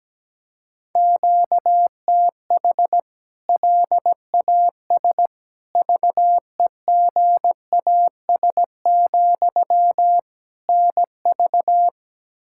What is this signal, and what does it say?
Morse code, heard as a keyed tone, about 17 wpm, 700 Hz. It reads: QTH LAS VEGAS, NV
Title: K1USN SST QSO Practice 17wpm